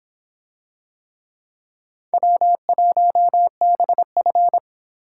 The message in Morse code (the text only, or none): W1BF